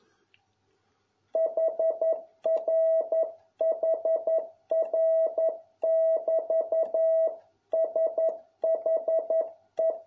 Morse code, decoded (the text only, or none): HRHR<BT>SHE